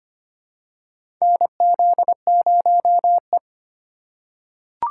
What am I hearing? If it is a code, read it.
Morse code, heard as a keyed tone, about 25 wpm, 700 Hz. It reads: NZ0E
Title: USA Call Signs - 4 Characters - 2x Character Spacing - 25wpm